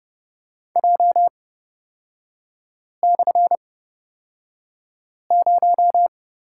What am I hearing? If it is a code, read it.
Morse code, heard as a keyed tone, about 30 wpm, spaced as for 5 wpm, 700 Hz. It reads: J/0
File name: Lesson 36e - BT - Three Characters 30wpm